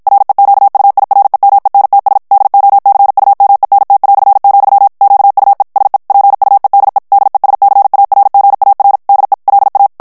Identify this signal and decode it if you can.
Morse code, heard as a keyed tone, about 56 wpm, 800 Hz. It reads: REQUIREMENTS DOCUMENT", QUE SE PUEDE DESCARGAR DE LA
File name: ESPAGNOL TEXTE EN CLAIR 1 VITESSE 50 MOTS MINUTES